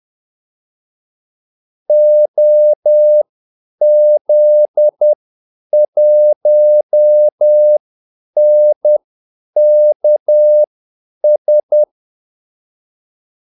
Morse code, heard as a keyed tone, about 10 wpm, spaced as for 8 wpm, 600 Hz. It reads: OZ1NKS
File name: Callsigns-10-8wpm